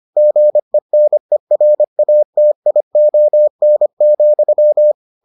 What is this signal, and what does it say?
Morse code, heard as a keyed tone, about 25 wpm, 600 Hz. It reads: GENERATION,